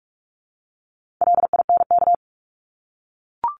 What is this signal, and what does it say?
Morse code, heard as a keyed tone, about 44 wpm, 700 Hz. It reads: LINK
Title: Top 751-1000 Words 45wpm